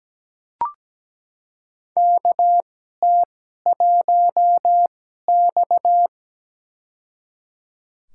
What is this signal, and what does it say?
Morse code, heard as a keyed tone, about 17 wpm, spaced as for 12 wpm, 700 Hz. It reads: KT1X
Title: USA Call Signs - 2x Character Spacing - 17wpm